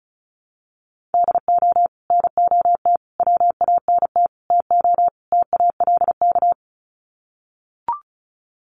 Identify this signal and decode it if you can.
Morse code, heard as a keyed tone, about 34 wpm, 700 Hz. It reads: DO NOT WANT TO TALK
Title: Sets of 5 Words 35wpm